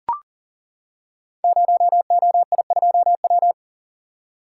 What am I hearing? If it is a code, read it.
Morse code, heard as a keyed tone, about 40 wpm, 700 Hz. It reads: 0OI2W
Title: Code Groups - 5 Letter:Numbers - 40wpm